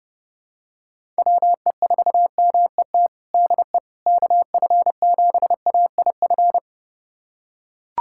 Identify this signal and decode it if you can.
Morse code, heard as a keyed tone, about 30 wpm, 700 Hz. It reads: WE4MET DE KF7AIF
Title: USA Callsign DE Callsign - 6 Characters - 30wpm